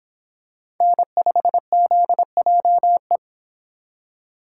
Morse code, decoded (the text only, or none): N5ZJE